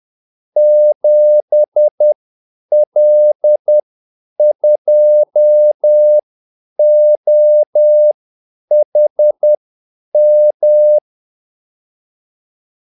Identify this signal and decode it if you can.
Morse code, heard as a keyed tone, about 10 wpm, spaced as for 8 wpm, 600 Hz. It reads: 7L2OHM
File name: Callsigns-10-8wpm